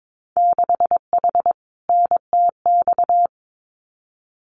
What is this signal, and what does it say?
Morse code, heard as a keyed tone, about 22 wpm, 700 Hz. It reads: 6H NTX
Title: Winter Field Day Exchanges Encore 22wpm